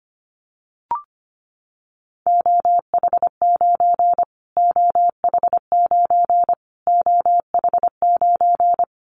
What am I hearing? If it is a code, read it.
Morse code, heard as a keyed tone, about 25 wpm, 700 Hz. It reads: OH9 OH9 OH9
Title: Callsign Prefixes for Top 25 DXCC - Repeated 3x - 25wpm